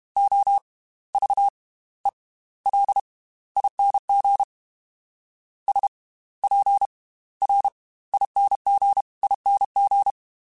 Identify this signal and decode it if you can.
Morse code, heard as a keyed tone, about 32 wpm, spaced as for 18 wpm, 800 Hz. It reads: OVELING SPRINGING